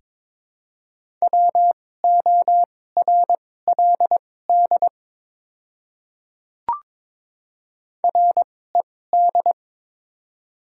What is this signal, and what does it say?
Morse code, heard as a keyed tone, about 23 wpm, spaced as for 16 wpm, 700 Hz. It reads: WORLD RED
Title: 2x Character Spacing - Top 500 Words - 22wpm